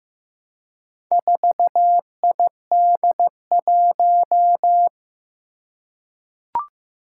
Morse code, heard as a keyed tone, about 15 wpm, 700 Hz. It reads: 4ID1
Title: Four Challenging Letter-Number Combinations 15wpm